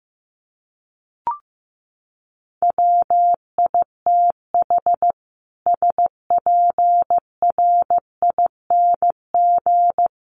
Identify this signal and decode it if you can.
Morse code, heard as a keyed tone, about 15 wpm, 700 Hz. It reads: WITH SPRING
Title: Sentences from Top 1200 Words 15wpm